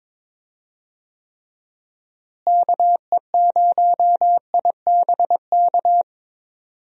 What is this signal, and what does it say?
Morse code, heard as a keyed tone, about 22 wpm, 700 Hz. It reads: KE0IBK
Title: USA Call Signs - 6 Characters - Repeated 2x - 22wpm